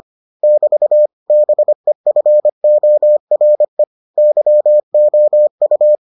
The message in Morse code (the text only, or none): <BT> BEFORE YOU